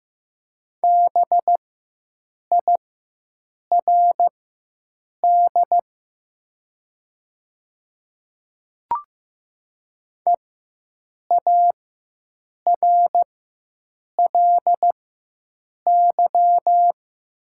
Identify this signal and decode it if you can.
Morse code, heard as a keyed tone, about 15 wpm, spaced as for 7 wpm, 700 Hz. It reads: BIRD EARLY
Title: 4x Character Spacing - Top 500 Words - 15wpm